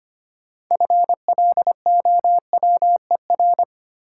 Morse code, heard as a keyed tone, about 25 wpm, 700 Hz. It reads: FLOWER
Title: Six Letter Words - 25wpm